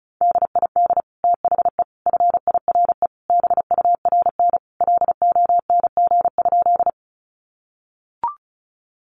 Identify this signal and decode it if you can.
Morse code, heard as a keyed tone, about 34 wpm, 700 Hz. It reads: DID THE FIRE BURN LONG?
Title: Sentences from Top 1300 Words 35wpm